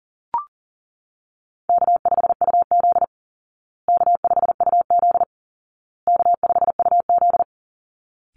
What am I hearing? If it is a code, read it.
Morse code, heard as a keyed tone, about 40 wpm, 700 Hz. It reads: K5UZ K5UZ K5UZ
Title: Easy Call Signs Repeated 3x - 40wpm